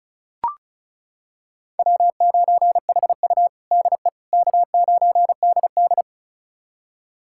Morse code, heard as a keyed tone, about 36 wpm, 700 Hz. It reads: W9HU DE K9DD
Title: USA Callsign DE Callsign - 4 Characters - 35wpm